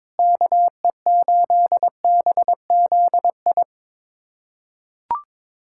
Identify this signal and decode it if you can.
Morse code, heard as a keyed tone, about 22 wpm, 700 Hz. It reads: KE8BZI
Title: USA Callsign DE Callsign 22wpm